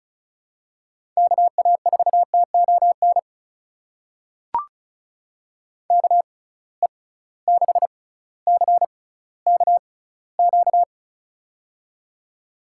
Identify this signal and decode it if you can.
Morse code, heard as a keyed tone, about 36 wpm, spaced as for 18 wpm, 700 Hz. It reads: KA4TON KE6CKQ
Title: USA Call Signs - 6 Characters - 6x Character Spacing - 35wpm